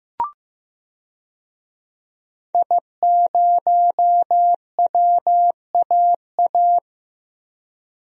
Morse code, heard as a keyed tone, about 15 wpm, 700 Hz. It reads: I0WAA
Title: DXCC-100 Call Signs - 5 Characters - Repeated 2x - 15wpm